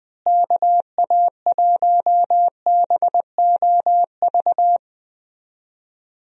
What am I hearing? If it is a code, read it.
Morse code, heard as a keyed tone, about 21 wpm, 700 Hz. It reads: KA1BOV
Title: USA Call Signs - 6 Characters - Repeated 3x - 20wpm